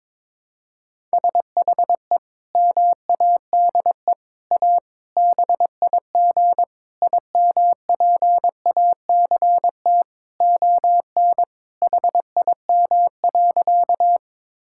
Morse code, heard as a keyed tone, about 22 wpm, 700 Hz. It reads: SHE MADE A BIG IMPACT ON HIM.
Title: Sentences from Top 1000 Words - Review Part 6 - 22wpm